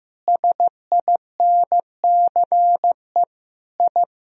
Morse code, heard as a keyed tone, about 15 wpm, 700 Hz. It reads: SINCE I
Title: Sentences from Top 1000 Words - Review Part 1 - 15wpm